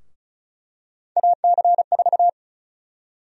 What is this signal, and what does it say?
Morse code, heard as a keyed tone, about 36 wpm, 700 Hz. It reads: AC4
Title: USA Call Sign Prefixes - Repeated 2x - 35wpm